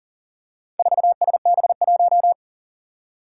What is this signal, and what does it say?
Morse code, heard as a keyed tone, about 37 wpm, 700 Hz. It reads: 4SB1
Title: Four Challenging Letter-Number Combinations 40wpm